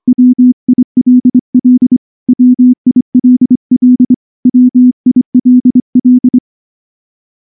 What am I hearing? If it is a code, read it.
Morse code, heard as a keyed tone, about 23 wpm, 250 Hz. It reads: WILL WILL WILL
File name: morse / morse-training-25WPM